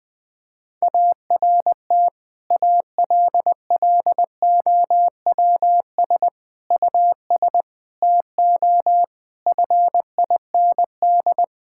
Morse code, heard as a keyed tone, about 21 wpm, 700 Hz. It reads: ART ALLOWS US TO FIND
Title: Difficult Sentences with Hints - 20wpm